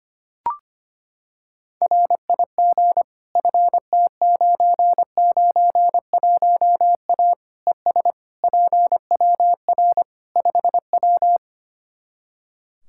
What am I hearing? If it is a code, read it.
Morse code, heard as a keyed tone, about 25 wpm, 700 Hz. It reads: RIG FT991A ES PWR 5W
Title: Practice QSO - Exchange 2 - Info - 25wpm